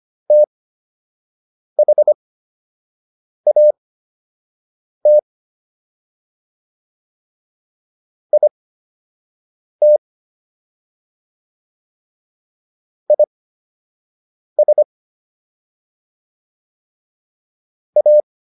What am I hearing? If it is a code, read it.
Morse code, heard as a keyed tone, about 25 wpm, spaced as for 6 wpm, 600 Hz. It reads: THAT IT IS A